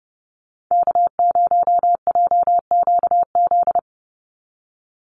Morse code, heard as a keyed tone, about 30 wpm, 700 Hz. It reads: K0JQZ
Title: NA SOTA Activator Callsigns 30wpm